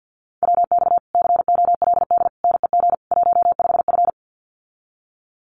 Morse code, heard as a keyed tone, about 50 wpm, 700 Hz. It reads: WX COLD NEG 15F
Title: Common QSO Phrases Encore 50wpm